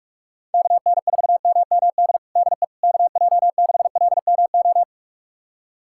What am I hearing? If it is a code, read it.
Morse code, heard as a keyed tone, about 46 wpm, 700 Hz. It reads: KN4MMD DE KJ6LMO
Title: USA Callsign DE Callsign - 6 Characters - 45wpm